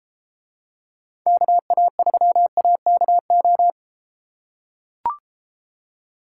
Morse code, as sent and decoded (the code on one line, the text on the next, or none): -.- .- ...-- .- -.- ---
KA3AKO